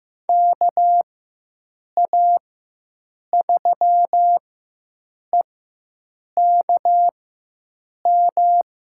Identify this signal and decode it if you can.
Morse code, heard as a keyed tone, about 15 wpm, spaced as for 7 wpm, 700 Hz. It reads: KA3EKM
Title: USA Call Signs - 6 Characters - 4x Character Spacing - 15wpm